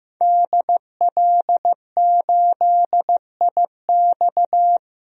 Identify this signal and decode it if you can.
Morse code, heard as a keyed tone, about 15 wpm, 700 Hz. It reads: DL8IX
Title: DXCC-100 Call Signs - 5 Characters - 15wpm